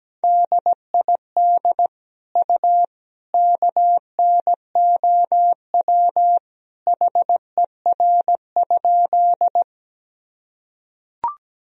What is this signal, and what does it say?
Morse code, heard as a keyed tone, about 17 wpm, 700 Hz. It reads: DID U KNOW HER?
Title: Sentences from Top 100 Words and Abbreviations 17wpm